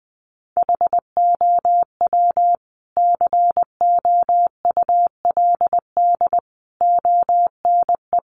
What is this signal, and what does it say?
Morse code, heard as a keyed tone, about 20 wpm, 700 Hz. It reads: HOW COULD ONE
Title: Sentences from Top 1100 Words 20wpm